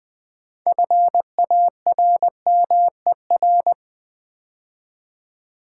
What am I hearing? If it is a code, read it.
Morse code, heard as a keyed tone, about 20 wpm, 700 Hz. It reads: FARMER
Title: 6 Letter Words - First 3 Letters Given - 20wpm